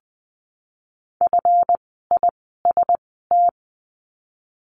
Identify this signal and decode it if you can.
Morse code, heard as a keyed tone, about 20 wpm, 700 Hz. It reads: F I S T
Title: 139 Lesson 23f - Letter f - Words 20wpm